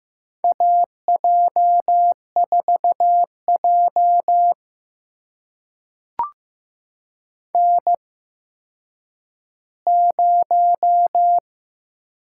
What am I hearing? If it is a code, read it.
Morse code, heard as a keyed tone, about 15 wpm, spaced as for 9 wpm, 700 Hz. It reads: AJ4J N0
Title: USA Call Signs - 4 Characters - 8x Character Spacing - 15wpm